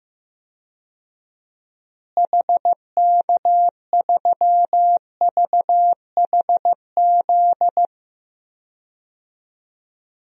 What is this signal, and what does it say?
Morse code, heard as a keyed tone, about 14 wpm, 700 Hz. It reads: HK3VHZ
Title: DXCC-100 Call Signs - 6 Characters - Repeated 2x - 15wpm